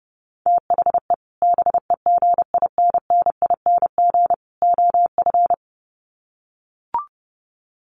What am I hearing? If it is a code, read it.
Morse code, heard as a keyed tone, about 30 wpm, 700 Hz. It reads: THE BEGINNING OF
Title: Sets of 3 Words 30wpm